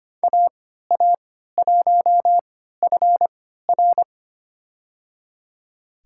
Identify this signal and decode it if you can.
Morse code, heard as a keyed tone, about 25 wpm, spaced as for 14 wpm, 700 Hz. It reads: AA1FR
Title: USA Call Signs - 5 Characters - 3x Character Spacing - 25wpm